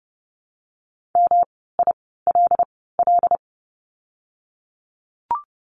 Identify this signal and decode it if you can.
Morse code, heard as a keyed tone, about 30 wpm, spaced as for 17 wpm, 700 Hz. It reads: MILL
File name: course / 30:10wpm / 108 Lesson 18f - Letter m - Words 30wpm